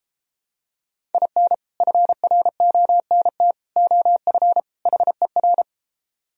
Morse code, heard as a keyed tone, about 33 wpm, 700 Hz. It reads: IN FRONT OF HER